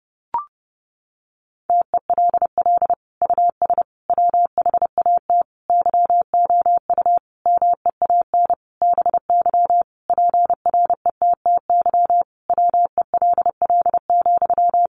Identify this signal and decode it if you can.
Morse code, heard as a keyed tone, about 30 wpm, 700 Hz. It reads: TELL US WHAT YOU MEAN BY PRETTY WELL,
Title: Sentences from Top 1000 Words 30wpm